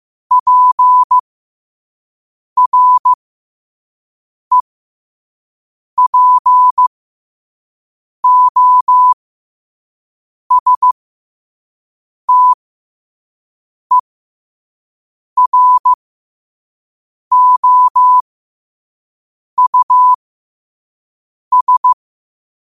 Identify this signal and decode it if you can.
Morse code, heard as a keyed tone, about 15 wpm, spaced as for 5 wpm, 1000 Hz. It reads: PREPOSTEROUS
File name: qotdfiles / 5wpm